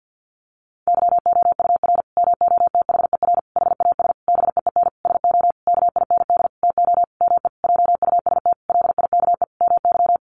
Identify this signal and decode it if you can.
Morse code, heard as a keyed tone, about 50 wpm, 700 Hz. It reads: YOUR MOTHER HAS BEEN SO KIND TO ME JUST LIKE MY
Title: Sentences from Top 1200 Words 50wpm